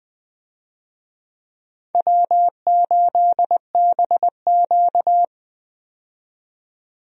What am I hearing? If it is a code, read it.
Morse code, heard as a keyed tone, about 21 wpm, 700 Hz. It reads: W8BQ
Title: USA Call Signs - 4 Characters - 20wpm